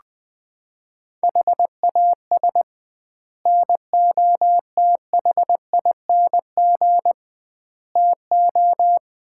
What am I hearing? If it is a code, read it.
Morse code, heard as a keyed tone, about 20 wpm, 700 Hz. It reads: HAS NOTHING TO
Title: 2x Word-Spacing - Sets of 5 Words - 20wpm